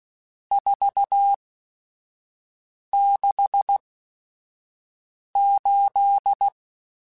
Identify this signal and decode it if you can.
Morse code, heard as a keyed tone, about 16 wpm, spaced as for 5 wpm, 800 Hz. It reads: 468
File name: RSTs / RST-mp3s / Various RST @5